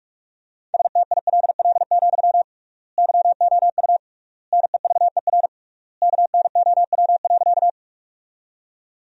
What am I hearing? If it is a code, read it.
Morse code, heard as a keyed tone, about 47 wpm, 700 Hz. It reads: STILL, YOU NEVER KNOW.
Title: 3x Word-Spacing - Sentences from Top 300 Words - 45wpm